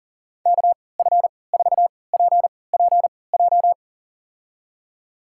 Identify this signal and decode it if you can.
Morse code, heard as a keyed tone, about 40 wpm, spaced as for 23 wpm, 700 Hz. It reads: KF4PPJ
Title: USA Call Signs - 3x Character Spacing - 40wpm